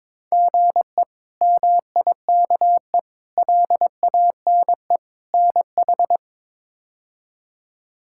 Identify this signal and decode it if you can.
Morse code, heard as a keyed tone, about 21 wpm, 700 Hz. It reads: GE MIKE LANE NH
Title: K1USN SST QSO Practice - 22wpm